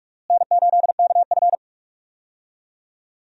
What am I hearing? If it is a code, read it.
Morse code, heard as a keyed tone, about 42 wpm, 700 Hz. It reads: N8KF
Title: USA Call Signs - 4 Characters - Repeated 2x - 45wpm